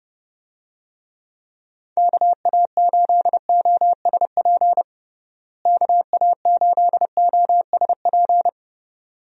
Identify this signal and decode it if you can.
Morse code, heard as a keyed tone, about 30 wpm, 700 Hz. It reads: KA8OSP KA8OSP
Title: USA Call Signs - 6 Characters - Repeated 2x - 30wpm